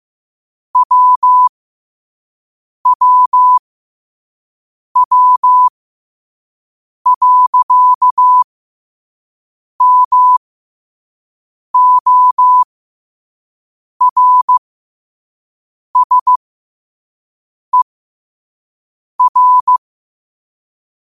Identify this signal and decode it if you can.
Morse code, heard as a keyed tone, about 15 wpm, spaced as for 5 wpm, 1000 Hz. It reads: WWW.MORSER
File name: qotdfiles / 5wpm